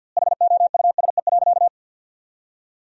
Morse code, heard as a keyed tone, about 52 wpm, 700 Hz. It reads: HOUSE.
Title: 2x Word-Spacing - Sentences from Top 200 Words - 50wpm